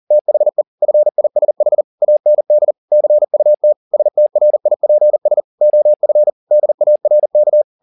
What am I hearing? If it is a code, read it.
Morse code, heard as a keyed tone, about 40 wpm, 600 Hz. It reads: THE FISH AND CUT STRIPS OF DARK